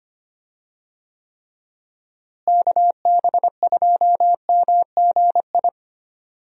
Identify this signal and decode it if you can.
Morse code, heard as a keyed tone, about 25 wpm, 700 Hz. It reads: KB2MGI
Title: USA Call Signs - 6 Characters - Repeated 2x - 25wpm